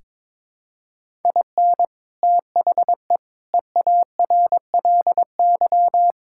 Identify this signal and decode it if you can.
Morse code, heard as a keyed tone, about 21 wpm, 700 Hz. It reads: IN THE EARLY
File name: Sentences with 7-Word Phrases - 22wpm